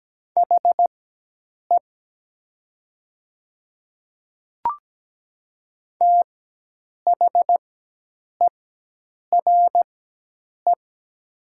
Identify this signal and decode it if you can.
Morse code, heard as a keyed tone, about 17 wpm, spaced as for 8 wpm, 700 Hz. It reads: HE THERE
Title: 4x Character Spacing - Top 100 Words - 17wpm